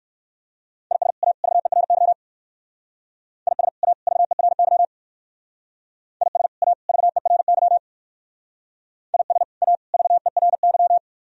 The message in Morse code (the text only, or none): IS A VERY IS A VERY IS A VERY IS A VERY